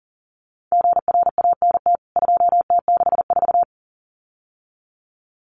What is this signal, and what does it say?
Morse code, heard as a keyed tone, about 40 wpm, 700 Hz. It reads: GRANT 2T64
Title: CWT Exchange 40wpm